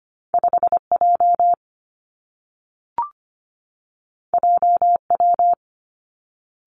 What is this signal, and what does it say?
Morse code, heard as a keyed tone, about 25 wpm, 700 Hz. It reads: 5 J J W
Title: Two Challenging Letter-Number Combinations 25wpm